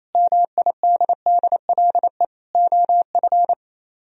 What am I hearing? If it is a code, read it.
Morse code, heard as a keyed tone, about 35 wpm, 700 Hz. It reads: MIDDLE OF